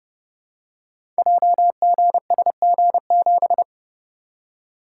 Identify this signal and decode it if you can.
Morse code, heard as a keyed tone, about 28 wpm, 700 Hz. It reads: JGSG7
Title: Code Groups - 5 Letter:Numbers - 30wpm